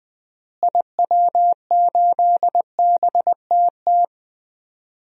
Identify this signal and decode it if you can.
Morse code, heard as a keyed tone, about 20 wpm, 700 Hz. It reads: IW8BTT
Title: DXCC-100 Call Signs - 6 Characters - Repeated 3x - 20wpm